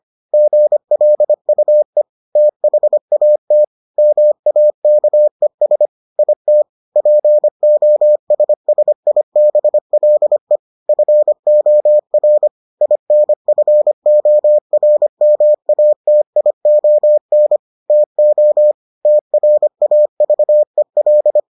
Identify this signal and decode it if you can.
Morse code, heard as a keyed tone, about 25 wpm, 600 Hz. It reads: GLUE THAT MAKES IT POSSIBLE FOR INFORMATION TO TRAVEL